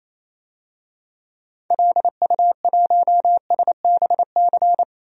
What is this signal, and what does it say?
Morse code, heard as a keyed tone, about 28 wpm, 700 Hz. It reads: LU1SBC
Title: DXCC-100 Call Signs - 6 Characters - 28wpm